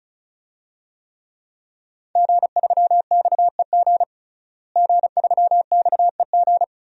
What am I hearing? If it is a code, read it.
Morse code, heard as a keyed tone, about 34 wpm, 700 Hz. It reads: G3XEG G3XEG
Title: DXCC-100 Call Signs - 5 Characters - Repeated 3x - 35wpm